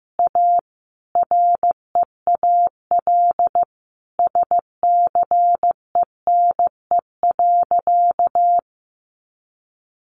Morse code, heard as a keyed tone, about 15 wpm, 700 Hz. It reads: A REAL SCENE.
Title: Sentences from Top 1100 Words 15wpm